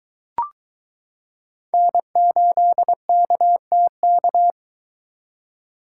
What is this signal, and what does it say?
Morse code, heard as a keyed tone, about 22 wpm, 700 Hz. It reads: N8KTK